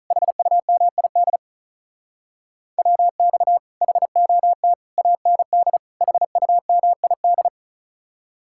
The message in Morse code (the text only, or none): HUMID WX HOT AND HUMID